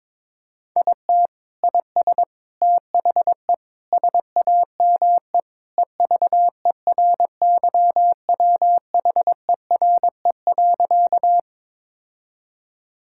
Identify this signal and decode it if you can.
Morse code, heard as a keyed tone, about 22 wpm, 700 Hz. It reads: IT IS THE SAME EVERYWHERE.
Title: Sentences from Top 1600 Words 22wpm